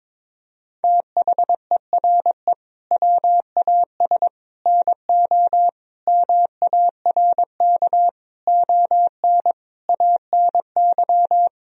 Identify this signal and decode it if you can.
Morse code, heard as a keyed tone, about 22 wpm, 700 Hz. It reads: THERE WAS NO MARK ON ANY
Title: Sentences from Top 1000 Words - Review Part 1 - 22wpm